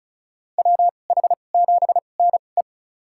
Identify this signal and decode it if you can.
Morse code, heard as a keyed tone, about 36 wpm, spaced as for 25 wpm, 700 Hz. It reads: WH7NE
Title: USA Call Signs - 5 Characters - 2x Character Spacing - 35wpm